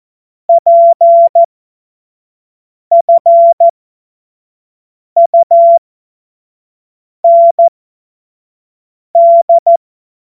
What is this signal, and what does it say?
Morse code, heard as a keyed tone, about 14 wpm, spaced as for 5 wpm, 680 Hz. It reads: PFUND